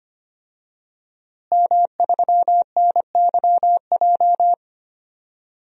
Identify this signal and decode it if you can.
Morse code, heard as a keyed tone, about 25 wpm, 700 Hz. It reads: M3NYJ
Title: DXCC-100 Call Signs - 5 Characters - Repeated 2x - 25wpm